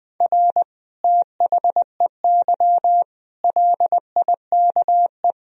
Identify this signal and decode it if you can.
Morse code, heard as a keyed tone, about 21 wpm, 700 Hz. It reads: R THEY LIKE
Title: Sentences from Top 200 Words and Abbreviations 20wpm